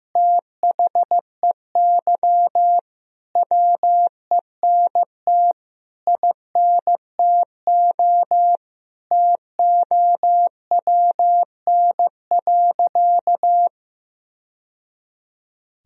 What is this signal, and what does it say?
Morse code, heard as a keyed tone, about 15 wpm, 700 Hz. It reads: THEY WENT INTO TOWN.
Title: Sentences from Top 1000 Words - Review Part 1 - 15wpm